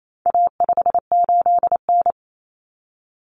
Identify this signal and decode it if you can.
Morse code, heard as a keyed tone, about 28 wpm, 700 Hz. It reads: A58N
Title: Common Cut Numbers - 4-Digit 28wpm